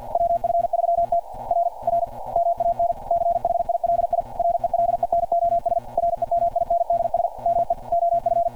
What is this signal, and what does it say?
Morse code, heard as a keyed tone, about 48 wpm, 690 Hz. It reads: WF3T R R MAN CHILI RELENOS REMINDS ME OF
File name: archived / CRSnet